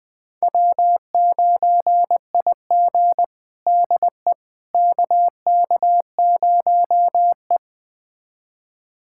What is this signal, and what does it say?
Morse code, heard as a keyed tone, about 20 wpm, 700 Hz. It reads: W9IG DE KK0E
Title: USA Callsign DE Callsign - 4 Characters - 20wpm